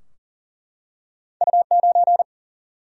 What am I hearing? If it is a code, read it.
Morse code, heard as a keyed tone, about 40 wpm, 700 Hz. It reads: U9
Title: Single-Letter Single-Number 40wpm